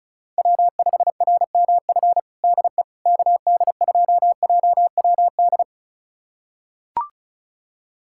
Morse code, heard as a keyed tone, about 34 wpm, 700 Hz. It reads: W5RMF DE KD2JWD
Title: USA Callsign DE Callsign 35wpm